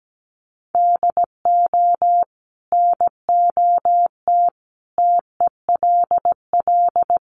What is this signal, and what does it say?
Morse code, heard as a keyed tone, about 17 wpm, 700 Hz. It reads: DO NOT TELL
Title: Sentences from Top 1500 Words 17wpm